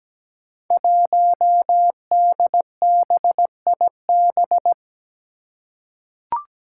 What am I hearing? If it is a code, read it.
Morse code, heard as a keyed tone, about 17 wpm, 700 Hz. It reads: 1DBIB
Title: Five Challenging Letter-Number Combinations 17wpm